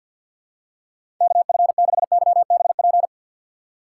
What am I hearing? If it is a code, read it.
Morse code, heard as a keyed tone, about 49 wpm, 700 Hz. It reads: KF6YBP
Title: USA Call Signs - 6x Character Spacing - 50wpm